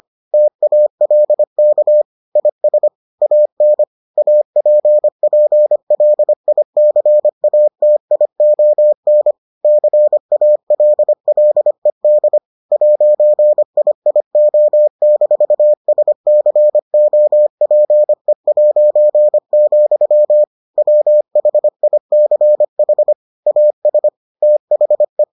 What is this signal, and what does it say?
Morse code, heard as a keyed tone, about 25 wpm, 600 Hz. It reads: TALK IS AN APPLICATION CALLED 'IIO-SCOPE', WHICH AS THE